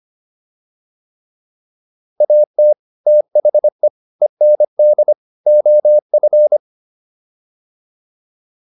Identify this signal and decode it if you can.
Morse code, heard as a keyed tone, about 25 wpm, 600 Hz. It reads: AT THE END OF